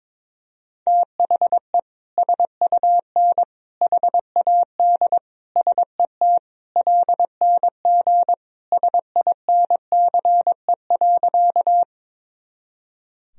Morse code, heard as a keyed tone, about 21 wpm, 700 Hz. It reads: THE SUN HAD SET LNG SINCE.
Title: Sentences from Top 300 Words and Abbreviations 22wpm